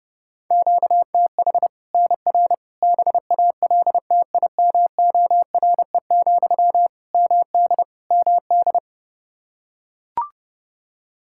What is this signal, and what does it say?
Morse code, heard as a keyed tone, about 32 wpm, 700 Hz. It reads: QTH NR BALTIMORE, MD MD
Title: Practice QSO - Exchange 1 - Beginning - 30wpm